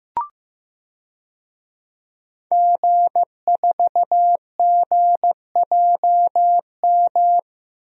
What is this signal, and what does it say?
Morse code, heard as a keyed tone, about 15 wpm, 700 Hz. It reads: G4GJM
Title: DXCC-100 Call Signs - 5 Characters - Repeated 2x - 15wpm